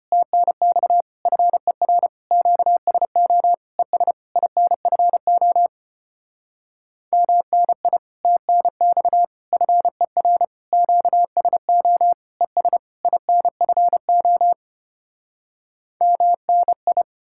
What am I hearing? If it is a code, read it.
Morse code, heard as a keyed tone, about 33 wpm, 700 Hz. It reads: TNX FER QSO ES INFO MNI TNX FER QSO ES INFO MNI